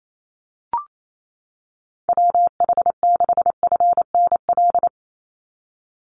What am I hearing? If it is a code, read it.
Morse code, heard as a keyed tone, about 28 wpm, 700 Hz. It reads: WH6FNL
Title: USA Call Signs 28wpm